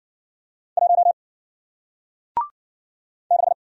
Morse code, heard as a keyed tone, about 53 wpm, 700 Hz. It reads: 1 6